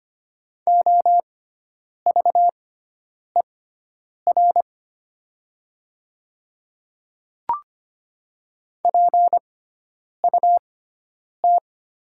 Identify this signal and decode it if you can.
Morse code, heard as a keyed tone, about 25 wpm, spaced as for 9 wpm, 700 Hz. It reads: OVER PUT
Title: 6x Character Spacing - Top 200 Words - 25wpm